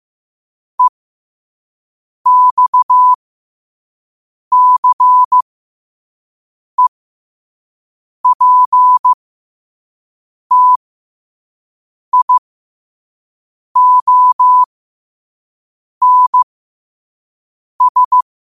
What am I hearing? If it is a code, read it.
Morse code, heard as a keyed tone, about 15 wpm, spaced as for 5 wpm, 1000 Hz. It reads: EXCEPTIONS
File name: qotdfiles / 5wpm